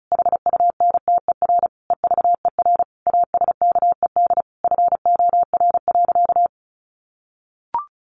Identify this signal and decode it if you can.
Morse code, heard as a keyed tone, about 34 wpm, 700 Hz. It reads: HUNTER EVER ASKED FOR.
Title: A Sound Of Thunder 35wpm